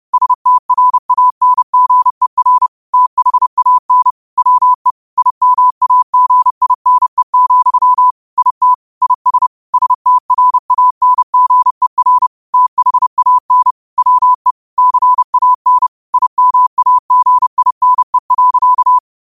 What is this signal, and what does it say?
Morse code, heard as a keyed tone, about 30 wpm, 1000 Hz. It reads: STRANGER THAN WE IMAGINE, IT IS STRANGER THAN WE CAN IMAGINE.
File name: qotdfiles / 30wpm